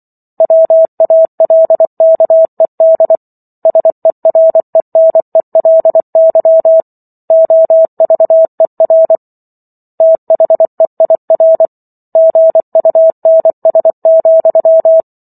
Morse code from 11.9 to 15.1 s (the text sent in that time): GUNS,